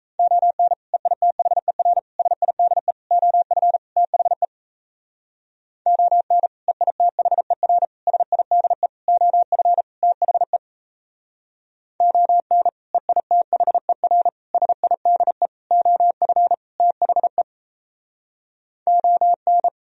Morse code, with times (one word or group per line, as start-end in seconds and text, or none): 0.2-0.7 ON
0.9-2.0 EITHER
2.2-2.9 SIDE
3.1-3.8 OF
4.0-4.5 THE
5.9-6.5 ON
6.7-7.9 EITHER
8.1-8.9 SIDE
9.1-9.8 OF
10.0-10.6 THE
12.0-12.7 ON
12.9-14.3 EITHER
14.5-15.5 SIDE
15.7-16.5 OF
16.8-17.4 THE
18.9-19.7 ON